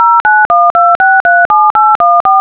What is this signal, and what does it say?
DTMF, heard as telephone touch tones, tones of about 230 ms. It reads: 0912637814
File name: audiosignalprocessing / example / dtmfExample